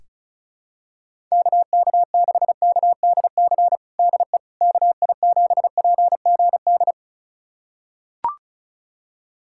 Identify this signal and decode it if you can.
Morse code, heard as a keyed tone, about 36 wpm, 700 Hz. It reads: KK6KDC DE KI7PGD
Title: USA Callsign DE Callsign 35wpm